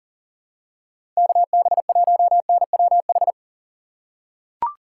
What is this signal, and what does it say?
Morse code, heard as a keyed tone, about 40 wpm, 700 Hz. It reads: KB1NWH
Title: USA Call Signs - 6 Characters - 6x Character Spacing - 40wpm